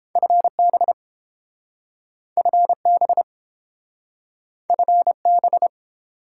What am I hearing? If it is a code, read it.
Morse code, heard as a keyed tone, about 30 wpm, 700 Hz. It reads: FB FB FB